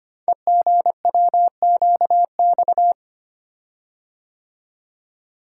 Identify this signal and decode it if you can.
Morse code, heard as a keyed tone, about 25 wpm, 700 Hz. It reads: EGWQX
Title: Code Groups - 5 Letters - 25wpm